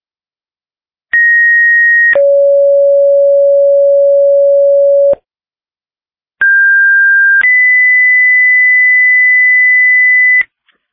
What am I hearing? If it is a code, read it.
Two-tone page, heard as a one-second tone, then a three-second tone, about 1900 then 560 Hz, about 1600 then 2000 Hz. It is heard from 1.1 to 10.4 s.